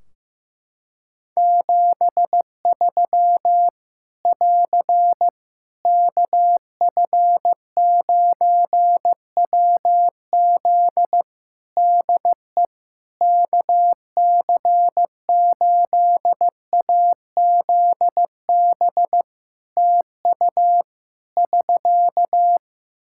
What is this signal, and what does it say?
Morse code, heard as a keyed tone, about 15 wpm, 700 Hz. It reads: 73 <AR> KF9WZ DE KC8AZB TU <SK>
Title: Practice QSO - Exchange 4 - Ending - 15wpm